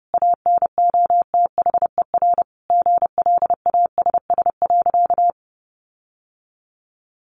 Morse code, heard as a keyed tone, about 30 wpm, 700 Hz. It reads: ANOTHER GLASS.
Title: Sentences from Top 1500 Words 30wpm